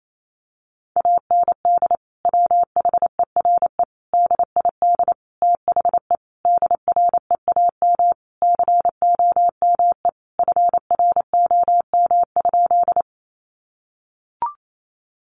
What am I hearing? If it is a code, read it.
Morse code, heard as a keyed tone, about 28 wpm, 700 Hz. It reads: AND WHERE DID THE DREAM COME FROM?
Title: Sentences from Top 1500 Words 28wpm